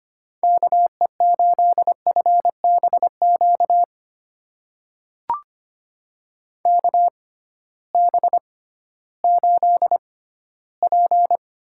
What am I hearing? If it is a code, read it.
Morse code, heard as a keyed tone, about 25 wpm, spaced as for 14 wpm, 700 Hz. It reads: KE8FBQ KB8P
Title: USA Call Signs - 6 Characters - 6x Character Spacing - 25wpm